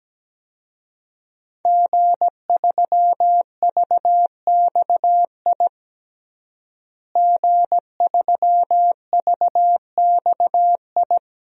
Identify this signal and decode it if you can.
Morse code, heard as a keyed tone, about 17 wpm, 700 Hz. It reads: G3VXI G3VXI
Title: DXCC-100 Call Signs - 5 Characters - Repeated 2x - 17wpm